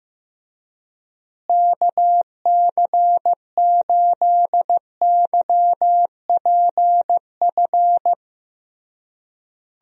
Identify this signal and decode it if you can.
Morse code, heard as a keyed tone, about 15 wpm, 700 Hz. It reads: KC8YPF
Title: USA Call Signs - 6 Characters - Repeated 3x - 15wpm